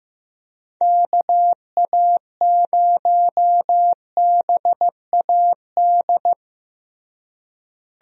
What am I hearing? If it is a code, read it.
Morse code, heard as a keyed tone, about 15 wpm, 700 Hz. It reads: KA0BAD
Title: USA Call Signs - 6 Characters - Repeated 2x - 15wpm